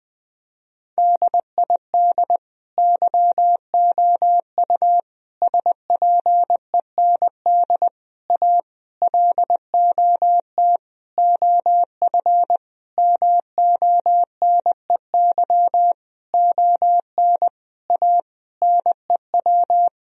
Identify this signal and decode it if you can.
Morse code, heard as a keyed tone, about 20 wpm, 700 Hz. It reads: DID YOU SPEND A LOT OF MONEY ON A NEW
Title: Sentences from Top 1000 Words - Review Part 4 - 20wpm